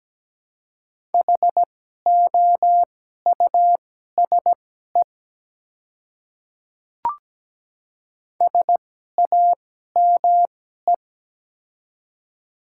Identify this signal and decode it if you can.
Morse code, heard as a keyed tone, about 17 wpm, spaced as for 12 wpm, 700 Hz. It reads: HOUSE SAME
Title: 2x Character Spacing - Top 200 Words - 17wpm